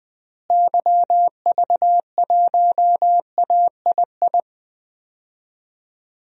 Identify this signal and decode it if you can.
Morse code, heard as a keyed tone, about 20 wpm, 700 Hz. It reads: YV1AII
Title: DXCC-100 Call Signs - 6 Characters - Repeated 2x - 20wpm